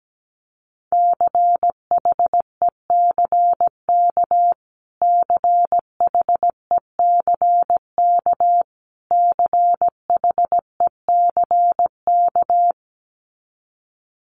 Morse code, heard as a keyed tone, about 17 wpm, 700 Hz. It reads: CHECK CHECK CHECK
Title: Top 500 Words Repeated 17wpm